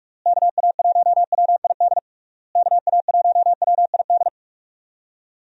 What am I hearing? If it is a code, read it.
Morse code, heard as a keyed tone, about 41 wpm, 700 Hz. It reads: KA1WID KA1WID
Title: USA Call Signs - 6 Characters - Repeated 3x - 45wpm